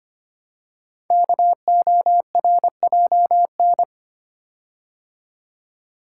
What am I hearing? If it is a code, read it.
Morse code, heard as a keyed tone, about 25 wpm, 700 Hz. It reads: KORJN
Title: Code Groups - 5 Letters - 25wpm